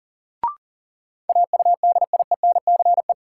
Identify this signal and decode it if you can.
Morse code, heard as a keyed tone, about 39 wpm, 700 Hz. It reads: AUDIENCE
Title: Sentences from Top 800 Words - New Word Drill - 40wpm